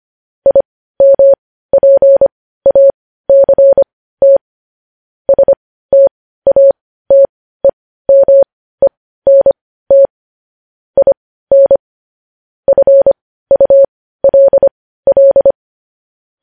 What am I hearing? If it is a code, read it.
Morse code, heard as a keyed tone, about 25 wpm, spaced as for 15 wpm, 550 Hz. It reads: IMPACT STATEMENT IN FULL
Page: News Headlines in Morse Code at 15 WPM